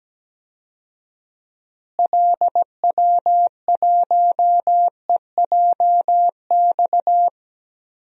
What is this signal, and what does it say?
Morse code, heard as a keyed tone, about 17 wpm, 700 Hz. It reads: LW1EJX
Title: DXCC-100 Call Signs - 6 Characters - Repeated 2x - 17wpm